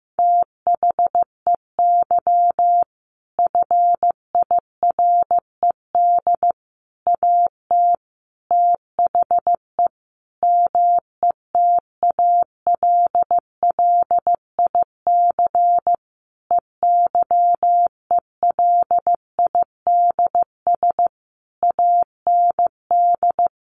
Morse code, heard as a keyed tone, about 15 wpm, 700 Hz. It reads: THEY FIRED AT THE METALLIC EYELIDS AND
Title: A Sound Of Thunder 15wpm